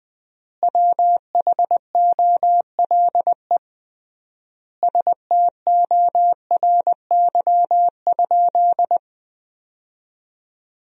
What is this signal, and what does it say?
Morse code, heard as a keyed tone, about 21 wpm, 700 Hz. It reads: WHOLE STORY?
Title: 3x Word-Spacing - Sentences from Top 400 Words - 20wpm